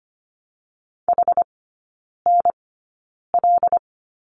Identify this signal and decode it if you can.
Morse code, heard as a keyed tone, about 25 wpm, spaced as for 9 wpm, 700 Hz. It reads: HNL
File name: Lesson 17e - Letter u - Three Characters 25wpm